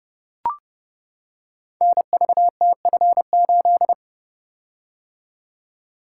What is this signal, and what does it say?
Morse code, heard as a keyed tone, about 30 wpm, 700 Hz. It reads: NVTF8
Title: Code Groups - 5 Letter:Numbers - 30wpm